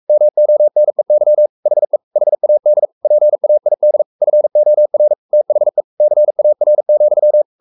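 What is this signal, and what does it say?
Morse code, heard as a keyed tone, about 44 wpm, 600 Hz. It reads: MONEY HE HAD PAID FOR THE CAR,
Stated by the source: Morse code at 43 wpm sending Ken Follett's novel Never in English, from CW med Gnister.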